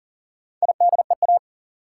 Morse code, heard as a keyed tone, about 40 wpm, 700 Hz. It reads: IDEA
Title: Sentences from Top 300 Words - New Word Drill - 40wpm